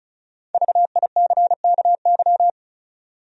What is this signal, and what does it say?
Morse code, heard as a keyed tone, about 36 wpm, 700 Hz. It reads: VICKY
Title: USA Common Names - Max 5 Letters - 35wpm